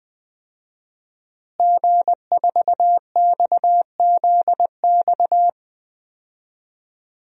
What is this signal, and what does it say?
Morse code, heard as a keyed tone, about 20 wpm, 700 Hz. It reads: G4XZX
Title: DXCC-100 Call Signs - 5 Characters - Repeated 3x - 20wpm